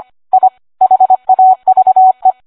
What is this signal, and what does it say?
Morse code, heard as a keyed tone, about 25 wpm, 750 Hz. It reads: I HAVE